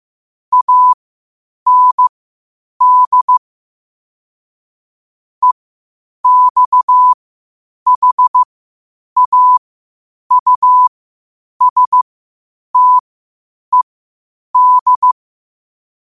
Morse code, heard as a keyed tone, about 15 wpm, spaced as for 8 wpm, 1000 Hz. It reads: AND EXHAUSTED
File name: qotdfiles / 8wpm